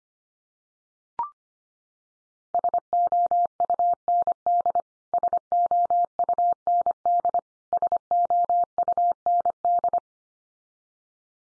Morse code, heard as a keyed tone, about 25 wpm, 700 Hz. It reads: SOUND SOUND SOUND
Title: Top 100 Words 25wpm x3